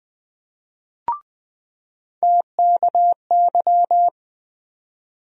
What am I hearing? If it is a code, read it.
Morse code, heard as a keyed tone, about 20 wpm, 700 Hz. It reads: TKY
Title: USA Call Sign Postfixes - 3 Letters 20wpm